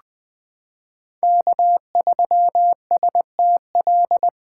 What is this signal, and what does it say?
Morse code, heard as a keyed tone, about 21 wpm, 700 Hz. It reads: K3STL
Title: 2023 POTA Hunter Call Signs - 20wpm